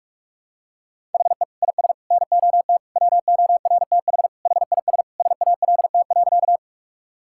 Morse code, heard as a keyed tone, about 46 wpm, 700 Hz. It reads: HE IS NOT WORTH HIS SALT.
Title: Sentences from Top 1000 Words - Review Part 5 - 45wpm